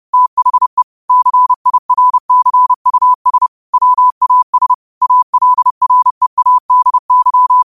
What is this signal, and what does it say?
Morse code, heard as a keyed tone, about 30 wpm, 1000 Hz. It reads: THE CIRCUS WAS ALREADY
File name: qotdfiles / 30wpm